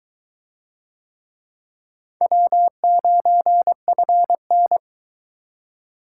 Morse code, heard as a keyed tone, about 21 wpm, 700 Hz. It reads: W9FN